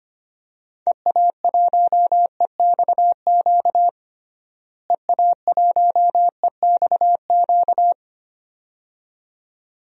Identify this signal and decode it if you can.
Morse code, heard as a keyed tone, about 25 wpm, 700 Hz. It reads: EA1EXQ EA1EXQ
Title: DXCC-100 Call Signs - 6 Characters - Repeated 2x - 25wpm